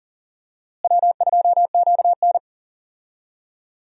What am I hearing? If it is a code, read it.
Morse code, heard as a keyed tone, about 37 wpm, 700 Hz. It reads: W2QN
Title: USA Call Signs - 4 Characters - Repeated 3x - 40wpm